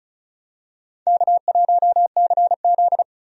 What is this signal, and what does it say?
Morse code, heard as a keyed tone, about 36 wpm, 700 Hz. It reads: K1CZ
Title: USA Call Signs - 4 Characters - 35wpm